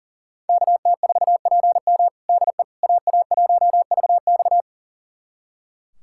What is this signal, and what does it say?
Morse code, heard as a keyed tone, about 40 wpm, 700 Hz. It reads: KT4PM DE AA1VX
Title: USA Callsign DE Callsign - 5 Characters - 40wpm